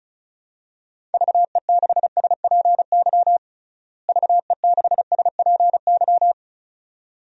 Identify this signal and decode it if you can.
Morse code, heard as a keyed tone, about 34 wpm, 700 Hz. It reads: VE6SPY VE6SPY
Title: DXCC-100 Call Signs - 6 Characters - Repeated 2x - 35wpm